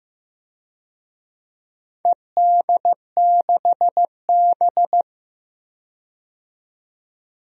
Morse code, heard as a keyed tone, about 15 wpm, 700 Hz. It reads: ED6B
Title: DXCC-100 Call Signs - 4 Characters - 15wpm